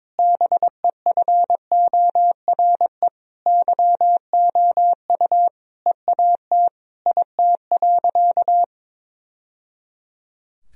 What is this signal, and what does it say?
Morse code, heard as a keyed tone, about 22 wpm, 700 Hz. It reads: BEFORE YOU EAT IT.
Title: Sentences from Top 1000 Words - Review Part 3 - 22wpm